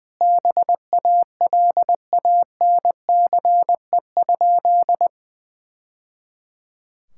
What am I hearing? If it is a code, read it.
Morse code, heard as a keyed tone, about 21 wpm, 700 Hz. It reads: BALANCE?
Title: Sentences from Top 1000 Words - Review Part 6 - 20wpm